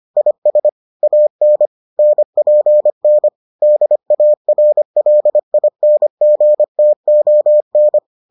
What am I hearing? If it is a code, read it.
Morse code, heard as a keyed tone, about 25 wpm, 600 Hz. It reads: IS AN NPN DARLINGTON